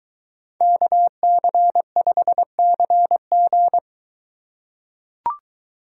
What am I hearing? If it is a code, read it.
Morse code, heard as a keyed tone, about 23 wpm, 700 Hz. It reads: KC5CG